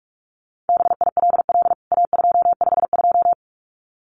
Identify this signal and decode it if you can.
Morse code, heard as a keyed tone, about 44 wpm, 700 Hz. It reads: BILL A252
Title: CWT Exchange 45wpm